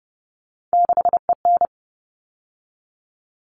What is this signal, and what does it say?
Morse code, heard as a keyed tone, about 30 wpm, 700 Hz. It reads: 6EN
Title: All Cut Numbers - 3-Digit 30wpm